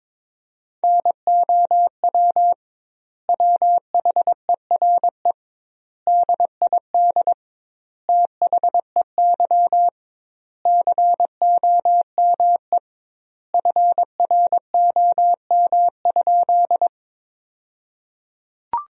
Morse code, heard as a keyed tone, about 22 wpm, 700 Hz. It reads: NOW WHERE DID THEY COME FROM?
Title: 2x Word-Spacing - Sentences from Top 200 Words - 22wpm